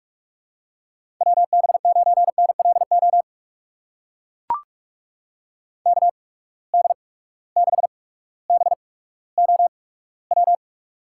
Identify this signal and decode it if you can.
Morse code, heard as a keyed tone, about 44 wpm, spaced as for 19 wpm, 700 Hz. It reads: WB9NLO KD6BOW
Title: USA Call Signs - 8x Character Spacing - 45wpm